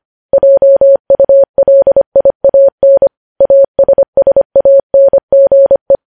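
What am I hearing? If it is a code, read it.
Morse code, heard as a keyed tone, about 25 wpm, 550 Hz. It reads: JULIAN ASSANGE
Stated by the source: Morse code news headlines at 25 words per minute.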